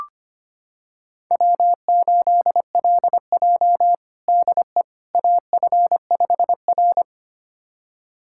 Morse code, heard as a keyed tone, about 25 wpm, 700 Hz. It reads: W8LJ DE AF5R
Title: USA Callsign DE Callsign - 4 Characters - 25wpm